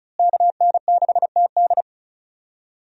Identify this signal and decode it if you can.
Morse code, heard as a keyed tone, about 34 wpm, 700 Hz. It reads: KN6TD
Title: USA Call Signs - 5 Characters - Repeated 2x - 35wpm